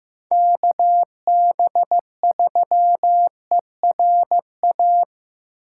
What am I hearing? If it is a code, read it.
Morse code, heard as a keyed tone, about 15 wpm, 700 Hz. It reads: KB3ERA
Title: USA Call Signs - 6x Character Spacing - 15wpm